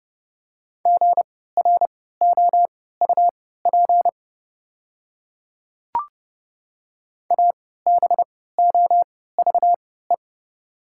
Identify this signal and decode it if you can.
Morse code, heard as a keyed tone, about 29 wpm, spaced as for 17 wpm, 700 Hz. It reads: GROUP ABOVE
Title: 3x Character Spacing - Top 500 Words - 30wpm